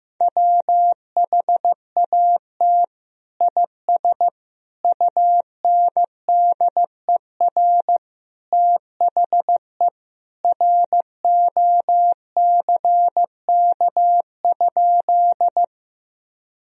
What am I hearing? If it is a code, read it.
Morse code, heard as a keyed tone, about 15 wpm, 700 Hz. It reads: WHAT IS UNDER THE ROCK?
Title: Sentences from Top 1000 Words - Review Part 2 - 15wpm